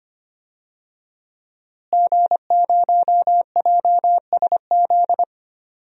Morse code, heard as a keyed tone, about 25 wpm, 700 Hz. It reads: G0JSZ
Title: DXCC-100 Call Signs - 5 Characters - Repeated 2x - 25wpm